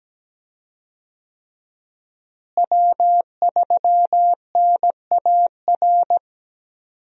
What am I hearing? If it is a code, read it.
Morse code, heard as a keyed tone, about 17 wpm, 700 Hz. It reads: W3NAR
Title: USA Call Signs - 5 Characters - Repeated 3x - 17wpm